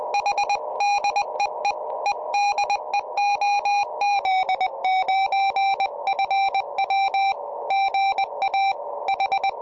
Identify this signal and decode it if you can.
Morse code, heard as a keyed tone, about 20 wpm, 780 Hz, with noise. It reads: H DEE EDE OZ9FW GA H
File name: tropo / oz9fw 70cm